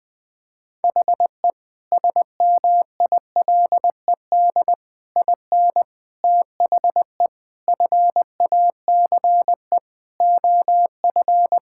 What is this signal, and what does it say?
Morse code, heard as a keyed tone, about 21 wpm, 690 Hz. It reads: HE SMILED IN THE FACE OF
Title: Sentences with 4-Word Phrases - Spoken Beginning - 20wpm